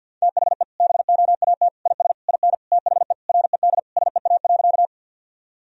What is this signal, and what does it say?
Morse code, heard as a keyed tone, about 50 wpm, 700 Hz. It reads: THE BOAT IS IN THE RED SEA.
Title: Sentences from Top 1000 Words - Review Part 2 - 50wpm